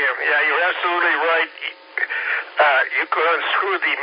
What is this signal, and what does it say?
Emergency Alert System (SAME) tones, heard as harsh buzzing bursts, with speech. It is heard from 0.2 to 3.1 s.